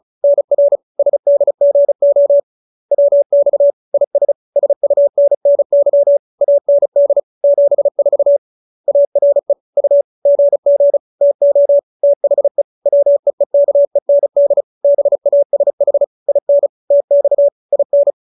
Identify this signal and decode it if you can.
Morse code, heard as a keyed tone, about 34 wpm, 570 Hz. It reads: NR SDGO WX IS SUNNY AND 74 ARE U GG TO THE WEEKEND BASH IN TX IN
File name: QSO312 35